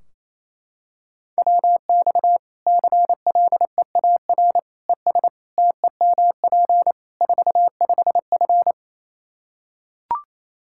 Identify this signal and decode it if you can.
Morse code, heard as a keyed tone, about 28 wpm, 700 Hz. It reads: WX CLEAR ES TEMP 45F
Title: Practice QSO - Exchange 2 - Info - 28wpm